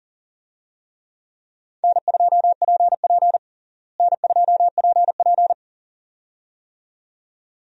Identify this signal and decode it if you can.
Morse code, heard as a keyed tone, about 40 wpm, 700 Hz. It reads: N2PP N2PP
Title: USA Call Signs - 4 Characters - Repeated 2x - 40wpm